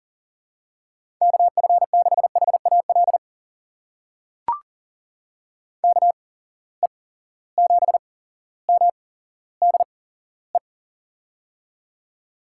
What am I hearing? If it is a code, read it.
Morse code, heard as a keyed tone, about 40 wpm, spaced as for 17 wpm, 700 Hz. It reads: KF6HAL KE7MDE
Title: USA Call Signs - 6 Characters - 8x Character Spacing - 40wpm